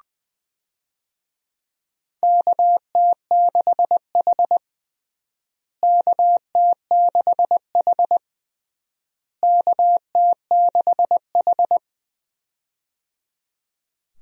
Morse code, heard as a keyed tone, about 20 wpm, 700 Hz. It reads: KT6H KT6H KT6H
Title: USA Call Signs - 4 Characters - Repeated 3x - 20wpm